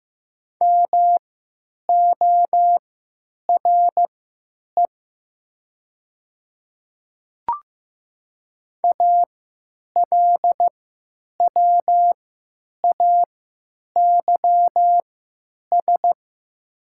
Morse code, heard as a keyed tone, about 15 wpm, spaced as for 8 wpm, 700 Hz. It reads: MORE ALWAYS
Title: 3x Character Spacing - Top 500 Words - 15wpm